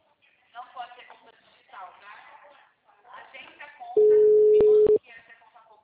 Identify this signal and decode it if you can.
Ringback tone (Europe, 425 Hz), with speech.
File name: out- -2006- - - .50609